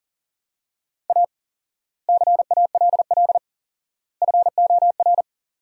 Morse code, heard as a keyed tone, about 40 wpm, 700 Hz. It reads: A CALL FOR
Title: 4x Word-Spacing - Sentences from Top 300 Words - 40wpm